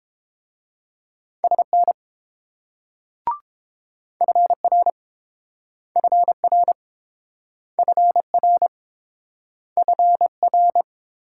Morse code, heard as a keyed tone, about 28 wpm, 700 Hz. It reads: SN FR FR FR FR